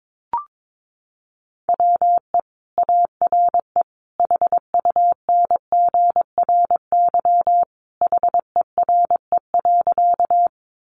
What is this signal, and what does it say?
Morse code, heard as a keyed tone, about 22 wpm, 700 Hz. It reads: WE ARE HUNGRY HERE.
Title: Sentences from Top 1400 Words 22wpm